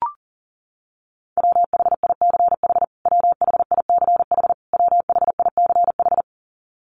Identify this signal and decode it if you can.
Morse code, heard as a keyed tone, about 40 wpm, 700 Hz. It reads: WHICH WHICH WHICH
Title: Top 500 Words Repeated 40wpm